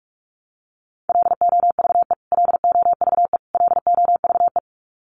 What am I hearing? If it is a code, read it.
Morse code, heard as a keyed tone, about 44 wpm, 700 Hz. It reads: LOVE LOVE LOVE
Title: Top 500 Words Repeated 45wpm